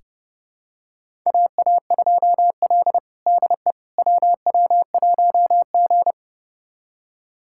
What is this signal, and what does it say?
Morse code, heard as a keyed tone, about 29 wpm, 700 Hz. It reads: AA2L DE WW1G
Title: USA Callsign DE Callsign - 4 Characters - 30wpm